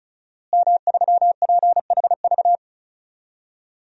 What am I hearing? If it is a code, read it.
Morse code, heard as a keyed tone, about 36 wpm, 700 Hz. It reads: M3PHV
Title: DXCC-100 Call Signs - 5 Characters - Repeated 2x - 35wpm